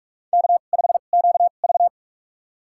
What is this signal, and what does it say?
Morse code, heard as a keyed tone, about 46 wpm, spaced as for 32 wpm, 690 Hz. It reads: K5QV
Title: USA Call Signs - 4 Characters - 2x Character Spacing - 45wpm